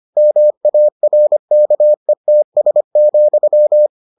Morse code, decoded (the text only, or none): MARKETS,